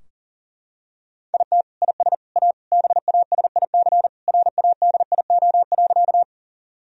Morse code, heard as a keyed tone, about 37 wpm, 700 Hz. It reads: IT IS A BASIC RADIO.
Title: Sentences from Top 1000 Words - Review Part 3 - 40wpm